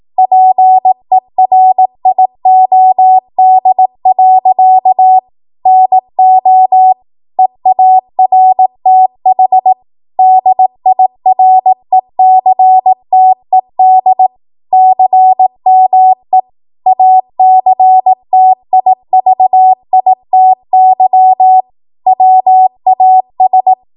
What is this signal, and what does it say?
Morse code, heard as a keyed tone, about 18 wpm, 750 Hz. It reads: PERIOD. NO EARTH DIRECTED CME ACTIVITY WAS
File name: Morse / 18WPM